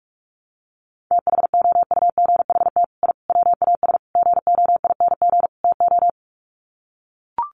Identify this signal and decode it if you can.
Morse code, heard as a keyed tone, about 44 wpm, 700 Hz. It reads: THOUGHT I WAS GOING TO
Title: Sets of 5 Words Encore2 45wpm